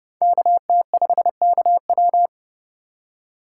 Morse code, heard as a keyed tone, about 28 wpm, 700 Hz. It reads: KT5KW